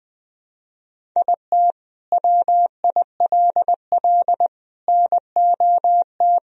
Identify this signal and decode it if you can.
Morse code, heard as a keyed tone, about 21 wpm, 700 Hz. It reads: IT WILL NOT
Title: Sentences from Top 1000 Words - Review Part 1 - 20wpm